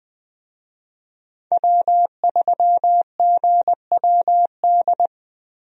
Morse code, heard as a keyed tone, about 20 wpm, 700 Hz. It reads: W3GWD
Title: USA Call Signs - 5 Characters - Repeated 2x - 20wpm